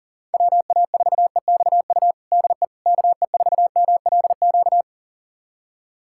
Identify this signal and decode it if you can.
Morse code, heard as a keyed tone, about 40 wpm, 700 Hz. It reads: WA4EXU DE KE4MLQ
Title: USA Callsign DE Callsign - 6 Characters - 40wpm